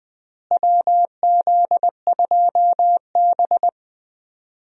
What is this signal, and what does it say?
Morse code, heard as a keyed tone, about 21 wpm, 700 Hz. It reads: WZ2B
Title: USA Call Signs - 4 Characters - Repeated 2x - 20wpm